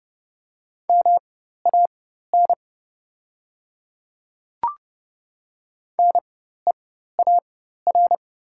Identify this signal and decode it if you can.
Morse code, heard as a keyed tone, about 30 wpm, spaced as for 14 wpm, 700 Hz. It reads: MAN NEAR
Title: 4x Character Spacing - Top 500 Words - 30wpm